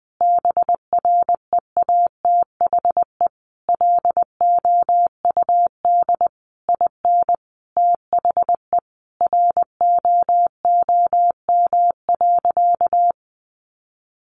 Morse code, heard as a keyed tone, about 20 wpm, 700 Hz. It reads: BREATHE LOUD IN THE ROOM.
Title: A Sound Of Thunder 20wpm